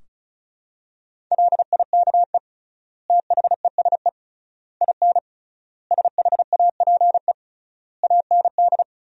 Morse code, heard as a keyed tone, about 34 wpm, 700 Hz. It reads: LIKE THESE IN SHAPE AND
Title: 3x Word-Spacing - Sentences from Top 500 Words - 35wpm